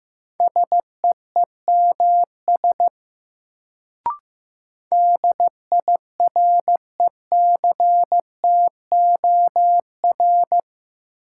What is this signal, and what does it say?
Morse code, heard as a keyed tone, about 15 wpm, 700 Hz. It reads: SEEMS DIRECTOR
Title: Sets of 3 Words - New Word Drill - 15wpm